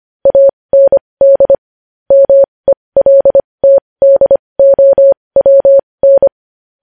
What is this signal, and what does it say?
Morse code, heard as a keyed tone, about 25 wpm, spaced as for 20 wpm, 550 Hz. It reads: AND MELTDOWN
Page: News Headlines in Morse Code at 20 WPM